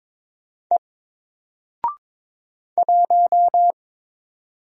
Morse code, heard as a keyed tone, about 22 wpm, 700 Hz. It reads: E 1
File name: Single-Letter-Number-22wpm